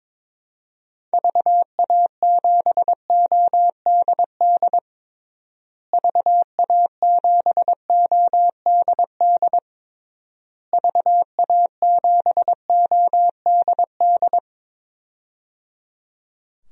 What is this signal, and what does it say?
Morse code, heard as a keyed tone, about 22 wpm, 700 Hz. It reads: VA7ODD VA7ODD VA7ODD
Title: DXCC-100 Call Signs - 6 Characters - Repeated 3x - 22wpm